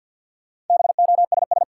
Morse code, heard as a keyed tone, about 47 wpm, 700 Hz. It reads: BOSS
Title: Sentences from Top 700 Words - New Word Drill - 50wpm